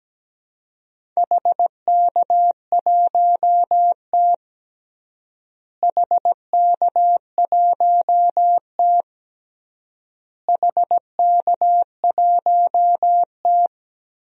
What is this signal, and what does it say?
Morse code, heard as a keyed tone, about 17 wpm, 700 Hz. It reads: HK1T HK1T HK1T
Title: DXCC-100 Call Signs - 4 Characters - Repeated 3x - 17wpm